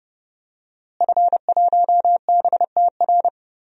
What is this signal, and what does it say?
Morse code, heard as a keyed tone, about 30 wpm, 700 Hz. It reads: F1BTR
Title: DXCC-100 Call Signs - 5 Characters - Repeated 3x - 30wpm